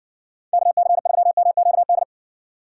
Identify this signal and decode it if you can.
Morse code, heard as a keyed tone, about 70 wpm, 700 Hz. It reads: KC3MQD